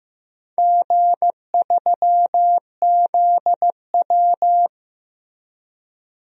Morse code, heard as a keyed tone, about 15 wpm, 700 Hz. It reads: G3ZW
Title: DXCC-100 Call Signs - 4 Characters - Repeated 3x - 15wpm